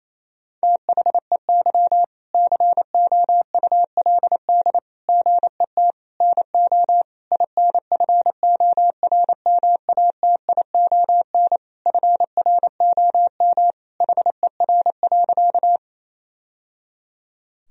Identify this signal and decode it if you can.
Morse code, heard as a keyed tone, about 27 wpm, 700 Hz. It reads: THEY COULD GET NO INFORMATION FROM HER.
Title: Sentences from Top 1000 Words - Review Part 3 - 28wpm